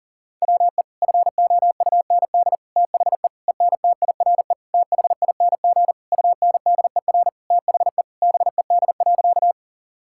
Morse code, heard as a keyed tone, about 37 wpm, 700 Hz. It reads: WE FOUND THE ENTIRE THING UNDER THE BED.
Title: Sentences from Top 1000 Words - Review Part 5 - 40wpm